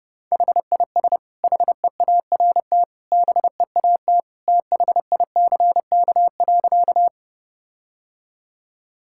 Morse code, heard as a keyed tone, about 30 wpm, 700 Hz. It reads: HIS HEART BEAT THICK.
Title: Sentences from Top 1600 Words 30wpm